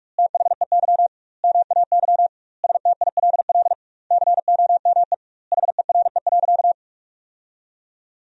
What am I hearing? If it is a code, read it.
Morse code, heard as a keyed tone, about 46 wpm, 700 Hz. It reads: THEY MAY STILL COME HERE.
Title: 2x Word-Spacing - Sentences from Top 300 Words - 45wpm